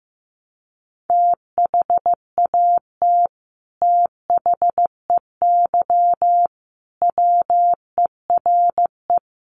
Morse code, heard as a keyed tone, about 15 wpm, 700 Hz. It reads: THAT THEY WERE
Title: Sets of 5 Words Encore 15wpm